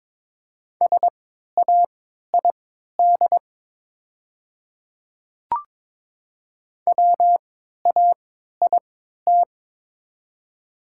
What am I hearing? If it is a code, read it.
Morse code, heard as a keyed tone, about 22 wpm, spaced as for 12 wpm, 700 Hz. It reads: SAID WAIT
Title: 3x Character Spacing - Top 500 Words - 22wpm